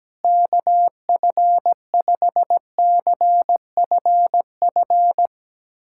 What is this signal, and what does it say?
Morse code, heard as a keyed tone, about 17 wpm, 700 Hz. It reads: KF5CFF
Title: USA Callsign DE Callsign 17wpm